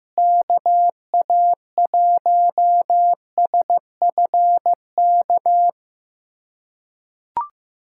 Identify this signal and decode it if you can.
Morse code, heard as a keyed tone, about 15 wpm, 700 Hz. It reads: KA1SFK
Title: USA Callsign DE Callsign 15wpm